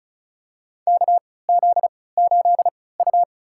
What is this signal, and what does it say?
Morse code, heard as a keyed tone, about 36 wpm, spaced as for 20 wpm, 700 Hz. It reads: KZ8U
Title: USA Call Signs - 4 Characters - 3x Character Spacing - 35wpm